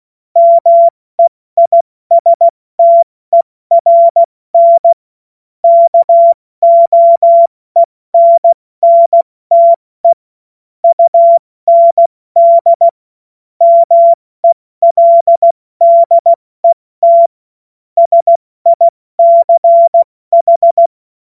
Morse code, heard as a keyed tone, about 16 wpm, spaced as for 14 wpm, 680 Hz. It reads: MEISTERN KOENNTE UND MELDET SICH
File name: Morsekurs / Witze / Witz 30 Januar 2024 70 F80